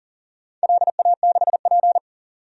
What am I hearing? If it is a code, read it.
Morse code, heard as a keyed tone, about 37 wpm, 700 Hz. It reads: LA6P
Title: DXCC-100 Call Signs - 4 Characters - 40wpm